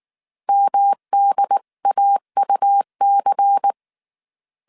Morse code, heard as a keyed tone, about 19 wpm, 800 Hz. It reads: MDAUC